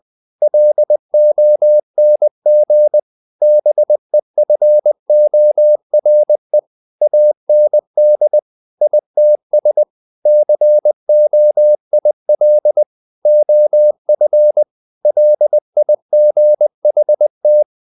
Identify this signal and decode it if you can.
Morse code, heard as a keyed tone, about 20 wpm, 600 Hz. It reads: LONG BEFORE AND ITS COIL OF LIGHT
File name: the old man and the sea part 2 20wpm